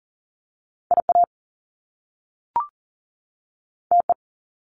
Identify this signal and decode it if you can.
Morse code, heard as a keyed tone, about 40 wpm, 700 Hz. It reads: IA TE
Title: Top Two Letter Combinations in Words 40wpm